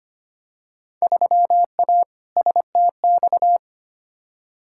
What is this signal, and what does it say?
Morse code, heard as a keyed tone, about 25 wpm, 700 Hz. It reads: 3A STX
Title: Field Day Exchanges - 25wpm